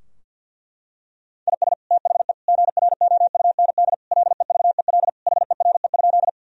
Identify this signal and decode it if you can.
Morse code, heard as a keyed tone, about 50 wpm, 700 Hz. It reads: IS THE GROUND LEVEL HERE?
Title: Sentences from Top 1000 Words - Review Part 4 - 50wpm